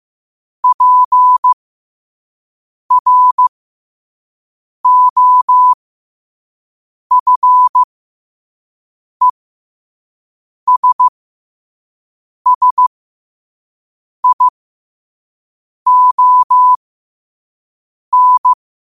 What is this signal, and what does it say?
Morse code, heard as a keyed tone, about 15 wpm, spaced as for 5 wpm, 1000 Hz. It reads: PROFESSION
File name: qotdfiles / 5wpm